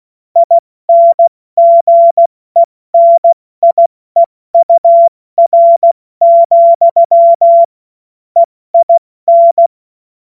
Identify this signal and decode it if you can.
Morse code, heard as a keyed tone, about 16 wpm, 680 Hz. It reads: INGENIEUR, EIN
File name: Morsekurs / Witze / Witz 14. Juni 2022 70 F80